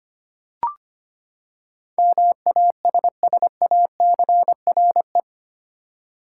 Morse code, heard as a keyed tone, about 25 wpm, 700 Hz. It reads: MASSACRE
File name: Eight Letter Words - 25wpm